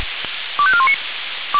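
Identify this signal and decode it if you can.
Five-tone selective call (ZVEI), heard as quick runs of five tones, tones of about 70 ms. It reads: 25419
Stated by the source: ZVEI five-tone call received on an SDR.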